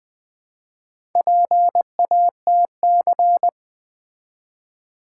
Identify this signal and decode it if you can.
Morse code, heard as a keyed tone, about 20 wpm, 700 Hz. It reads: PATC
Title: Build A Word - 1 Letter At A Time - 5-Letter Words - 20wpm